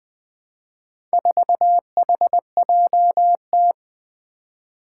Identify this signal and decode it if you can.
Morse code, heard as a keyed tone, about 20 wpm, 700 Hz. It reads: 4HJT
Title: Four Challenging Letter-Number Combinations 20wpm